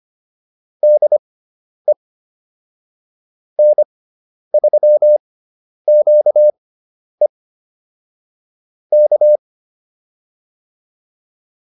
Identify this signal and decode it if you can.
Morse code, heard as a keyed tone, about 25 wpm, spaced as for 10 wpm, 600 Hz. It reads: DE N3QE K